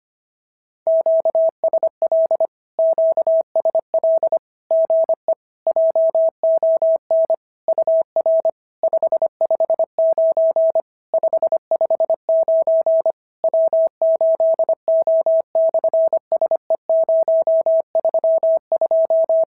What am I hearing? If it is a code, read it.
Morse code, heard as a keyed tone, about 25 wpm, 650 Hz. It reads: QSL QSL GE JON UR 559 559 W8O/SE032
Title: SOTA QSO Practice 25wpm